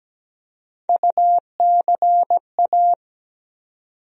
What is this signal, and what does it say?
Morse code, heard as a keyed tone, about 17 wpm, 700 Hz. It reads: UCA
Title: USA Call Sign Postfixes - 3 Letters 17wpm